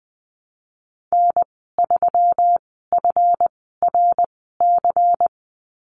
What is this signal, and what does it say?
Morse code, heard as a keyed tone, about 19 wpm, spaced as for 14 wpm, 700 Hz. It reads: N3FRC
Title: 173 Lesson 28g - Number 9 - Call Signs 20wpm